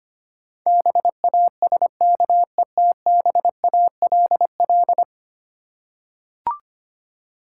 Morse code, heard as a keyed tone, about 25 wpm, 700 Hz. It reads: BASKETBALL
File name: Ten Letter Words - 25wpm